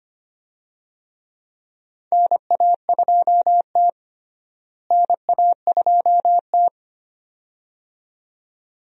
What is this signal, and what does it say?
Morse code, heard as a keyed tone, about 25 wpm, 700 Hz. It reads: NA2T NA2T
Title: USA Call Signs - 4 Characters - Repeated 2x - 25wpm